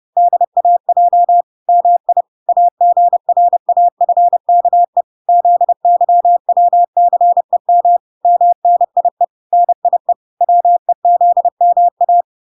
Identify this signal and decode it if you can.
Morse code, heard as a keyed tone, about 30 wpm, 700 Hz. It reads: DAJ MI AGRAFKE ZYWCEM MNIE NIE WEZMA